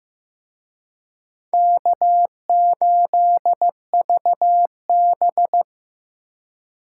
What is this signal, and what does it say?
Morse code, heard as a keyed tone, about 18 wpm, 700 Hz. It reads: K8VB
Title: Speed Racing - USA 1x2 Call Signs 15wpm